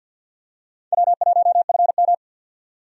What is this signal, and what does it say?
Morse code, heard as a keyed tone, about 46 wpm, 700 Hz. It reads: W1FM
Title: USA Call Signs - 4 Characters - 50wpm